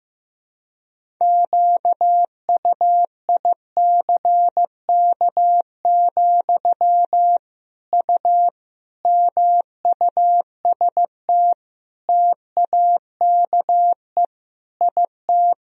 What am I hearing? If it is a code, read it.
Morse code, heard as a keyed tone, about 15 wpm, 700 Hz. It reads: QUICK, U MUST TAKE IT
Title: Sentences from Top 500 Words and Abbreviations 15wpm